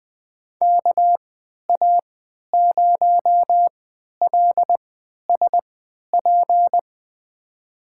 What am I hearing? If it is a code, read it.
Morse code, heard as a keyed tone, about 20 wpm, spaced as for 11 wpm, 700 Hz. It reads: KA0LSP
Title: USA Call Signs - 6 Characters - 3x Character Spacing - 20wpm